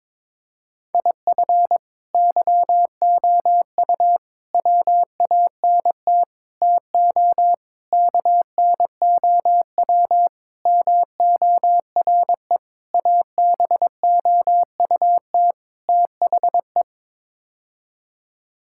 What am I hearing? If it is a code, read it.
Morse code, heard as a keyed tone, about 22 wpm, 700 Hz. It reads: IF YOU WANT TO KNOW MORE ABOUT THE
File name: practice / 22wpm / Sets of 8 Words - 22wpm